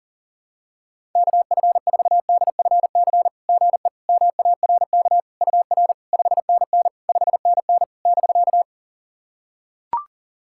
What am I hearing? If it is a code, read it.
Morse code, heard as a keyed tone, about 44 wpm, 690 Hz. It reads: KF4DFC GE MARK UR 5NN 5NN <BK>
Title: SOTA QSO Practice 40wpm